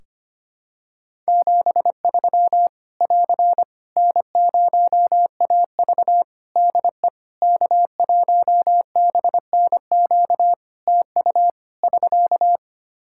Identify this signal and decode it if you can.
Morse code, heard as a keyed tone, about 25 wpm, 700 Hz. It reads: 73 <AR> N0AV DE K1BNQ TU <SK>
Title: Practice QSO - Exchange 4 - Ending - 25wpm